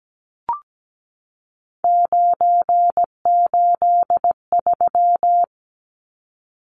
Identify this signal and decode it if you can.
Morse code, heard as a keyed tone, about 17 wpm, 700 Hz. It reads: 983
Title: Numbers - 3-Digit 17wpm